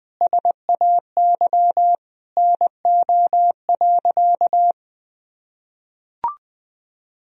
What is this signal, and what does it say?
Morse code, heard as a keyed tone, about 21 wpm, 700 Hz. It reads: SAY NO.
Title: Sentences from Top 1000 Words - Review Part 4 - 20wpm